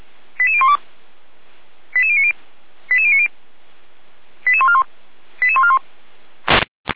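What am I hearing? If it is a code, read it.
Five-tone selective call (ZVEI), heard as quick runs of five tones, tones of about 70 ms. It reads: 88012 88090 88090 88141 88141
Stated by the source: Five-tone ZVEI selective call, received with an SDR, recorded by feuerwehr-orlamuende (fire brigade Orlamünde).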